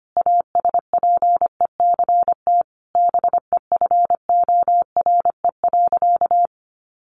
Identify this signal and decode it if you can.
Morse code, heard as a keyed tone, about 25 wpm, 700 Hz. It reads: ASPECT BEFORE.
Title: Sentences from Top 1300 Words 25wpm